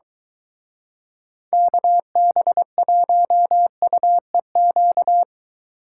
Morse code, heard as a keyed tone, about 23 wpm, 700 Hz. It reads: KB1UEQ